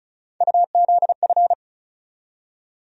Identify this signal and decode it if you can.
Morse code, heard as a keyed tone, about 36 wpm, 700 Hz. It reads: UZF
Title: USA Call Sign Postfixes - 3 Letters 35wpm